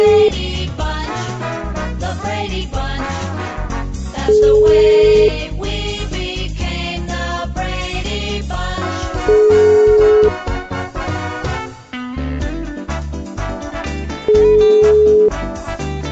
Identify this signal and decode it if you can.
Ringback tone (Europe, 425 Hz), with noise.